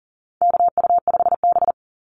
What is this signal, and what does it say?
Morse code, heard as a keyed tone, about 37 wpm, 700 Hz. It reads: KU5B
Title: Call Signs Repeated 2x - 40wpm